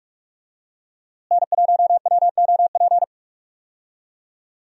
Morse code, heard as a keyed tone, about 47 wpm, 700 Hz. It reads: N1WOP
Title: USA Call Signs - 5 Characters - Repeated 3x - 45wpm